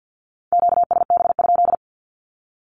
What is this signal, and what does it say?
Morse code, heard as a keyed tone, about 51 wpm, 700 Hz. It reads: QSB?
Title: Q-Codes 50wpm